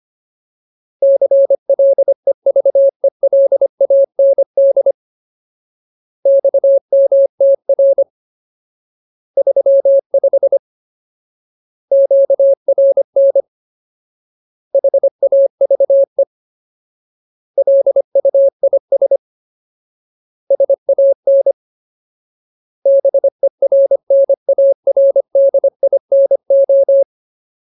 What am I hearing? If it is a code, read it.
Morse code, heard as a keyed tone, about 25 wpm, 550 Hz. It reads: CLEVELAND XMTR 35 QRN HAVE LUIS SAN BERNARDINO
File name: Ultimate Word List / 25WPM Ultimate